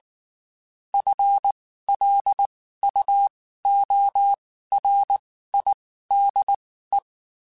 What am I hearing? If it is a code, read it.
Morse code, heard as a keyed tone, about 19 wpm, spaced as for 14 wpm, 800 Hz. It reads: FLUORIDE
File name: Spaced / FL Runs sp mp3s / FL Runs sp @18